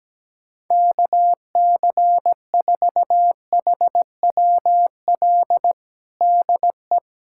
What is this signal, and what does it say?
Morse code, heard as a keyed tone, about 17 wpm, 700 Hz. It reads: KC4HWL DE
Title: USA Callsign DE Callsign 17wpm